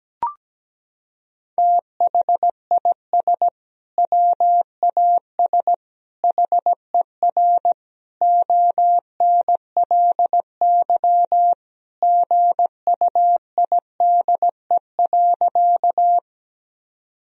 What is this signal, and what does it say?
Morse code, heard as a keyed tone, about 17 wpm, 700 Hz. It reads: THIS WAS HER ONLY GUIDE.
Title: Sentences from Top 1000 Words - Review Part 3 - 17wpm